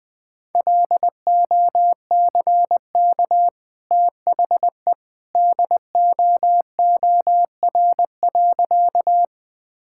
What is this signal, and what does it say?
Morse code, heard as a keyed tone, about 21 wpm, 700 Hz. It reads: LOCK THE DOOR.
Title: Easy Sentences with Hints - 20wpm